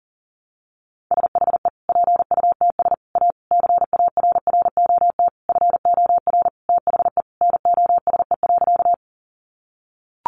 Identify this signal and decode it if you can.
Morse code, heard as a keyed tone, about 40 wpm, 700 Hz. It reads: SHE PUTS A CARROT FOR THE NOSE.
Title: Easy Phrases 40wpm